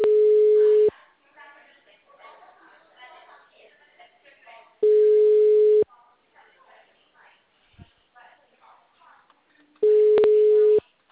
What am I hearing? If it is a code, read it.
Ringback tone (Europe, 425 Hz), with speech.